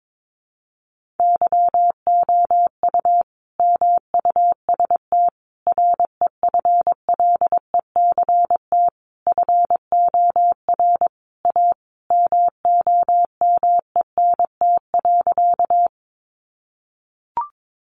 Morse code, heard as a keyed tone, about 22 wpm, 700 Hz. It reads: YOU MUST REFLECT FOR A MOMENT.
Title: Sentences from Top 1500 Words 22wpm